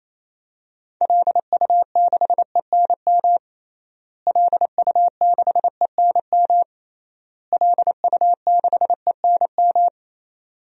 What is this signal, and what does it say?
Morse code, heard as a keyed tone, about 28 wpm, 700 Hz. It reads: LU6ENM LU6ENM LU6ENM
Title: DXCC-100 Call Signs - 6 Characters - Repeated 3x - 28wpm